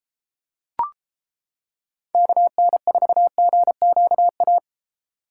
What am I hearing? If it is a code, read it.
Morse code, heard as a keyed tone, about 32 wpm, 700 Hz. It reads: KN4GQA